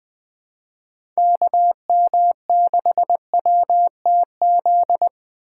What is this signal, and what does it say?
Morse code, heard as a keyed tone, about 22 wpm, 700 Hz. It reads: KM6WTZ
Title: Speed Racing - USA Call Signs 20wpm